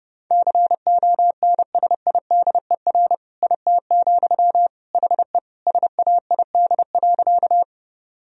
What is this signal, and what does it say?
Morse code, heard as a keyed tone, about 30 wpm, 700 Hz. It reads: CONSIDER IT, HE SAID.
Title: Sentences from Top 1000 Words - Review Part 4 - 30wpm